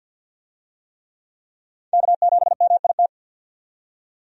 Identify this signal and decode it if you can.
Morse code, heard as a keyed tone, about 51 wpm, 700 Hz. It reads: K7MIT
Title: USA Call Signs - 5 Characters - 50wpm